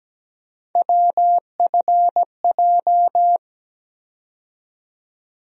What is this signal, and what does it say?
Morse code, heard as a keyed tone, about 17 wpm, 700 Hz. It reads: WFJ
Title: USA Call Sign Postfixes - 3 Letters 17wpm